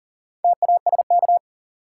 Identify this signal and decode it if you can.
Morse code, heard as a keyed tone, about 40 wpm, 700 Hz. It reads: TASK
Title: Sentences from Top 1100 Words - New Word Drill Alt - 40wpm